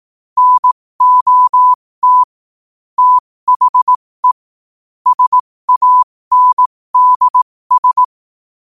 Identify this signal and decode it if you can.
Morse code, heard as a keyed tone, about 18 wpm, spaced as for 15 wpm, 1000 Hz. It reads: NOT THE SANDS